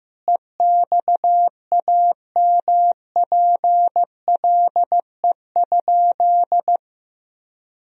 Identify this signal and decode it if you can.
Morse code, heard as a keyed tone, about 15 wpm, 700 Hz. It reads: EXAMPLE?
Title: Sentences from Top 1000 Words - Review Part 1 - 15wpm